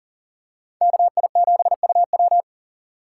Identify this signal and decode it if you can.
Morse code, heard as a keyed tone, about 38 wpm, 700 Hz. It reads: KI7UW
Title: USA Call Signs - 2x Character Spacing - 40wpm